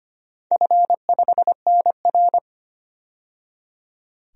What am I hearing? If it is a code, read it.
Morse code, heard as a keyed tone, about 25 wpm, 700 Hz. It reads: F5NR
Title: DXCC-100 Call Signs - 4 Characters - 25wpm